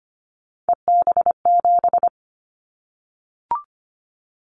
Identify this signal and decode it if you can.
Morse code, heard as a keyed tone, about 25 wpm, 700 Hz. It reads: EB7
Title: Callsign Prefixes for Top 25 DXCC - 25wpm